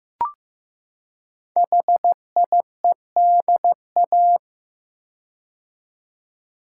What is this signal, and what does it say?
Morse code, heard as a keyed tone, about 15 wpm, 700 Hz. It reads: HIEDA
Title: Code Groups - 5 Letters - 15wpm